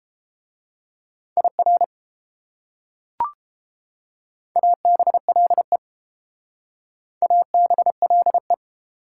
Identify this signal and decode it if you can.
Morse code, heard as a keyed tone, about 32 wpm, 700 Hz. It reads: IR ABLE ABLE